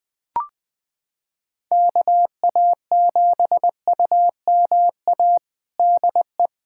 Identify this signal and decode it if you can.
Morse code, heard as a keyed tone, about 21 wpm, 700 Hz. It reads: KA7UMA DE
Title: USA Callsign DE Callsign - 6 Characters - 20wpm